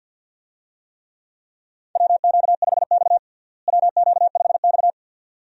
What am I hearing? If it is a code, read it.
Morse code, heard as a keyed tone, about 48 wpm, 700 Hz. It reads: WQ5X WQ5X
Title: USA Call Signs - 4 Characters - Repeated 2x - 50wpm